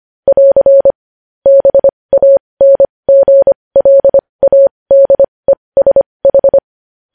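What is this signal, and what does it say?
Morse code, heard as a keyed tone, about 25 wpm, spaced as for 20 wpm, 550 Hz. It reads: <AR> BANGLADESH
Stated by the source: Morse code news headlines sent at 20 wpm.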